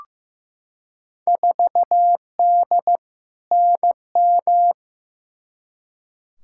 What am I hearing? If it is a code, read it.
Morse code, heard as a keyed tone, about 15 wpm, 700 Hz. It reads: 4D NM
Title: Field Day Exchanges Encore - 15wpm